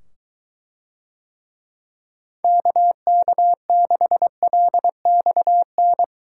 Morse code, heard as a keyed tone, about 23 wpm, 700 Hz. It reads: KK6LXN